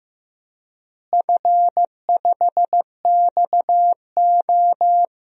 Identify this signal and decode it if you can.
Morse code, heard as a keyed tone, about 16 wpm, 700 Hz. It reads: F5XO
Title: DXCC-100 Call Signs - 4 Characters - Repeated 2x - 15wpm